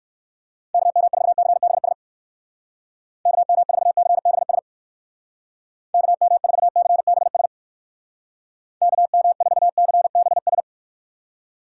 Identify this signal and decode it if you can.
Morse code, heard as a keyed tone, about 59 wpm, 700 Hz. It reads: KM4CBS KM4CBS KM4CBS KM4CBS